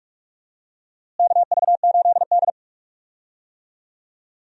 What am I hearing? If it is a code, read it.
Morse code, heard as a keyed tone, about 46 wpm, 700 Hz. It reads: KV8D
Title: USA Call Signs - 4 Characters - Repeated 2x - 45wpm